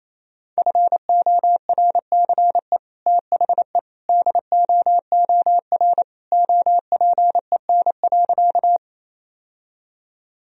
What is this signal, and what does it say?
Morse code, heard as a keyed tone, about 28 wpm, 700 Hz. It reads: FORCE THE DOOR OPEN.
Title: Sentences from Top 1000 Words - Review Part 3 - 28wpm